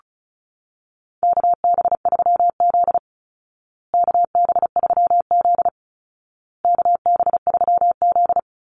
Morse code, heard as a keyed tone, about 34 wpm, 700 Hz. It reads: KB3Z KB3Z KB3Z
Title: Call Signs Repeated 2x - 35wpm